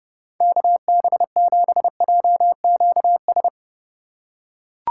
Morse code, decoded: KB7JQS

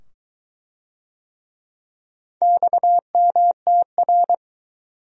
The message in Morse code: XMTR

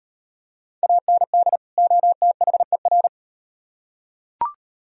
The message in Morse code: AND OTHER

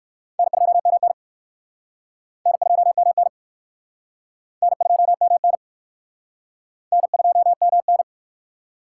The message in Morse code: N2MN N2MN N2MN N2MN

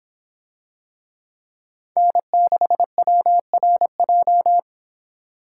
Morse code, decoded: N6WRJ

